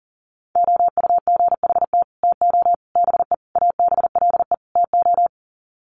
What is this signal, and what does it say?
Morse code, heard as a keyed tone, about 40 wpm, 700 Hz. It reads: OUGHT TO BE ABLE TO
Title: Sets of 5 Words Encore2 40wpm